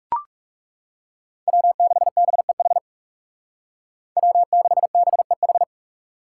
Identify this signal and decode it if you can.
Morse code, heard as a keyed tone, about 35 wpm, 700 Hz. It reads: W6BEH W6BEH
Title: Speed Racing - USA 1x3 Call Signs 30wpm